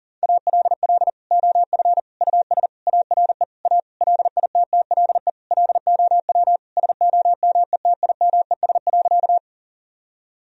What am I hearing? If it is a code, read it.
Morse code, heard as a keyed tone, about 40 wpm, 700 Hz. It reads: ALL OF US ARE A LITTLE LOW SOMETIMES.
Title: Sentences from Top 1000 Words - Review Part 3 - 40wpm